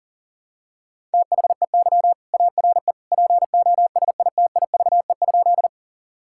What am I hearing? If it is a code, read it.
Morse code, heard as a keyed tone, about 40 wpm, 700 Hz. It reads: THEY ARE POSITIVE?
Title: Sentences from Top 1000 Words - Review Part 5 - 40wpm